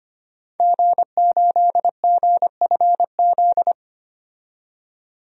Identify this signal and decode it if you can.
Morse code, heard as a keyed tone, about 25 wpm, 700 Hz. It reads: G8GFZ
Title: DXCC-100 Call Signs - 5 Characters - 25wpm